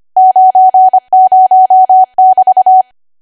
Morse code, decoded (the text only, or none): 90<BT>